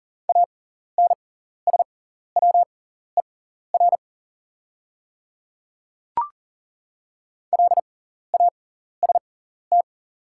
Morse code, heard as a keyed tone, about 39 wpm, spaced as for 14 wpm, 700 Hz. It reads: ANSWER LAST
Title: 6x Character Spacing - Top 300 Words - 40wpm